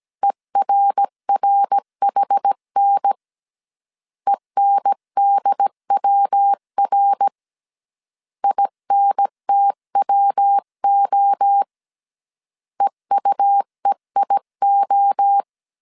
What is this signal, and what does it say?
Morse code, heard as a keyed tone, about 17 wpm, 800 Hz. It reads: ERRHN ENDWR INTWO EUEIO